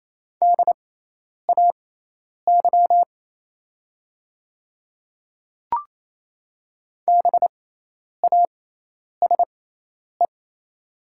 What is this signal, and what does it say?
Morse code, heard as a keyed tone, about 27 wpm, spaced as for 10 wpm, 690 Hz. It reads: DAY BASE
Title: 6x Character Spacing - Top 300 Words - 28wpm